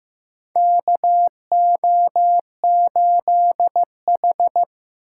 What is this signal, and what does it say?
Morse code, heard as a keyed tone, about 15 wpm, 700 Hz. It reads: KO8H
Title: USA Call Signs - 6x Character Spacing - 15wpm